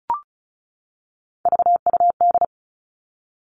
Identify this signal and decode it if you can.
Morse code, heard as a keyed tone, about 36 wpm, 700 Hz. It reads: VUD
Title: Three Challenging Letter-Number Combinations 35wpm